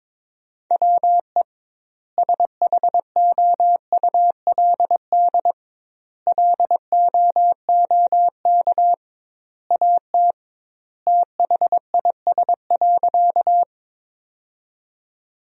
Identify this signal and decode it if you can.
Morse code, heard as a keyed tone, about 21 wpm, 700 Hz. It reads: WE SHOULD LOOK AT THIS.
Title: 2x Word-Spacing - Sentences from Top 300 Words - 22wpm